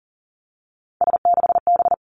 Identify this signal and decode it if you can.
Morse code, heard as a keyed tone, about 38 wpm, 700 Hz. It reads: S6B
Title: Three Challenging Letter-Number Combinations 40wpm